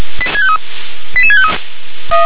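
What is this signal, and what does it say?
Five-tone selective call (ZVEI), heard as quick runs of five tones, tones of about 70 ms. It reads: 88653
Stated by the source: ZVEI five-tone call received on SDR.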